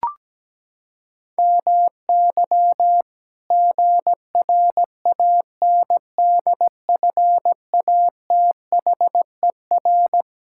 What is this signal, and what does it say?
Morse code, heard as a keyed tone, about 17 wpm, 700 Hz. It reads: MY GRANDFATHER
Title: Pangram Challenge v2 - 17wpm